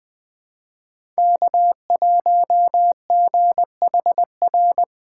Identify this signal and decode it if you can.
Morse code, heard as a keyed tone, about 20 wpm, 700 Hz. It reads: K1GHR
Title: USA Call Signs - 5 Characters - 6x Character Spacing - 20wpm